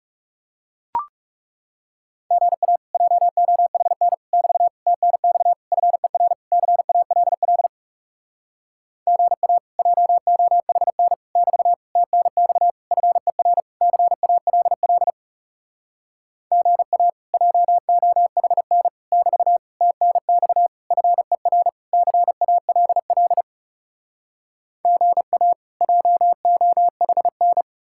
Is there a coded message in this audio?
Yes: Morse code from 2.3 to 27.6 s, heard as a keyed tone, about 40 wpm, 700 Hz. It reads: GA JOHN <BT> TNX FER CALL GA JOHN <BT> TNX FER CALL GA JOHN <BT> TNX FER CALL GA JOHN